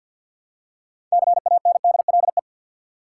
Morse code, heard as a keyed tone, about 47 wpm, 700 Hz. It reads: CANDLE
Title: Six Letter Words - 50wpm